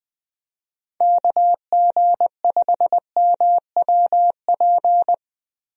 Morse code, heard as a keyed tone, about 21 wpm, 700 Hz. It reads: KG5MWP
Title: USA Call Signs - 6 Characters - Repeated 3x - 20wpm